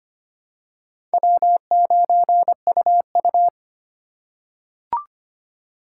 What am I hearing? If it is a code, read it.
Morse code, heard as a keyed tone, about 25 wpm, 700 Hz. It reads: W9UU
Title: USA Call Signs - 4 Characters - 8x Character Spacing - 25wpm